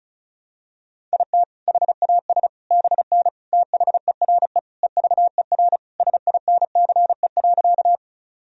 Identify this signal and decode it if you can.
Morse code, heard as a keyed tone, about 36 wpm, 700 Hz. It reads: IT HAS BN THERE EVER SINCE.
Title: Sentences from Top 400 Words and Abbreviations 35wpm